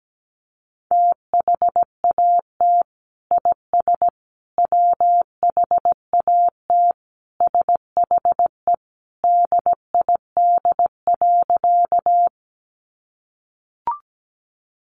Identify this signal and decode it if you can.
Morse code, heard as a keyed tone, about 17 wpm, 700 Hz. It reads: THAT IS WHAT SHE DID.
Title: Sentences from Top 100 Words 17wpm